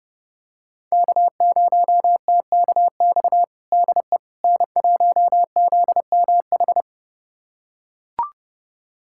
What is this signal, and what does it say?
Morse code, heard as a keyed tone, about 32 wpm, 700 Hz. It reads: K0TKX DE N1ZMH
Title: USA Callsign DE Callsign - 5 Characters - 30wpm